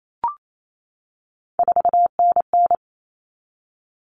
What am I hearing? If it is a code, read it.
Morse code, heard as a keyed tone, about 28 wpm, 700 Hz. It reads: 4NN
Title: Common Cut Numbers - 3-Digit 28wpm